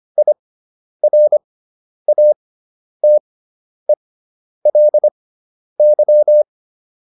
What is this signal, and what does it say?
Morse code, heard as a keyed tone, about 25 wpm, spaced as for 10 wpm, 600 Hz. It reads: IRATELY